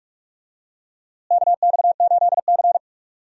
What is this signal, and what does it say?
Morse code, heard as a keyed tone, about 43 wpm, 700 Hz. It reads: KX8C
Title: USA Call Signs - 4 Characters - Repeated 2x - 45wpm